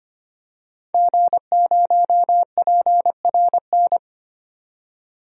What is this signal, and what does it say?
Morse code, heard as a keyed tone, about 25 wpm, 700 Hz. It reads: G0PRN
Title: DXCC-100 Call Signs - 5 Characters - 25wpm